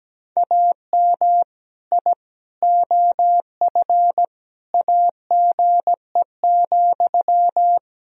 Morse code, heard as a keyed tone, about 17 wpm, 700 Hz. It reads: AM I OF AGE,
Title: Sentences from Top 1000 Words - Review Part 3 - 17wpm